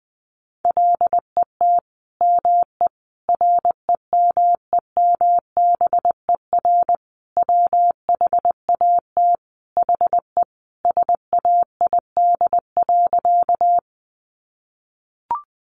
Morse code, heard as a keyed tone, about 19 wpm, 700 Hz. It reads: LET ME REMEMBER WHAT HE SAID.
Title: Sentences from Top 400 Words 20wpm